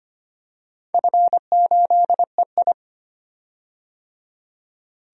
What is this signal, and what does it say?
Morse code, heard as a keyed tone, about 25 wpm, 700 Hz. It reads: F8EI
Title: DXCC-100 Call Signs - 4 Characters - 25wpm